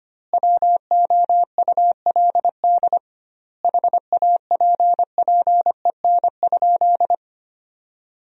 Morse code, heard as a keyed tone, about 25 wpm, 700 Hz. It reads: WOULD HAPPEN?
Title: 2x Word-Spacing - Sentences from Top 400 Words - 25wpm